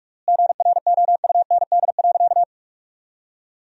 Morse code, heard as a keyed tone, about 46 wpm, 700 Hz. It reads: GROUND.
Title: 3x Word-Spacing - Sentences from Top 500 Words - 45wpm